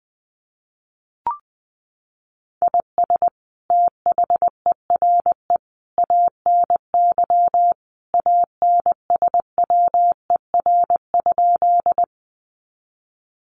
Morse code, heard as a keyed tone, about 20 wpm, 700 Hz. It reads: IS THERE ANY ANSWER?
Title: Sentences from Top 300 Words 20wpm